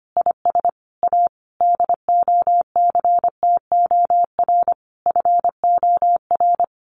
Morse code, heard as a keyed tone, about 25 wpm, 700 Hz. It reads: IS A DOCTOR FOR
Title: Additional Easy Phrases 25wpm